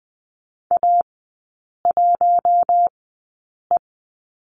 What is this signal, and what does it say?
Morse code, heard as a keyed tone, about 20 wpm, spaced as for 8 wpm, 700 Hz. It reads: A1E
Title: 041 Lesson 8e - Number 1 - Three Characters 20wpm